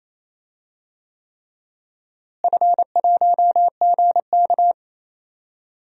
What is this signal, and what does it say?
Morse code, heard as a keyed tone, about 28 wpm, 700 Hz. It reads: F1GK
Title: DXCC-100 Call Signs - 4 Characters - Repeated 3x - 28wpm